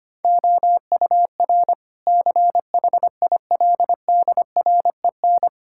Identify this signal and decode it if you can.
Morse code, heard as a keyed tone, about 25 wpm, 700 Hz. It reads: OUR CHILDREN
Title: Sentences from Top 1000 Words - Review Part 5 - 25wpm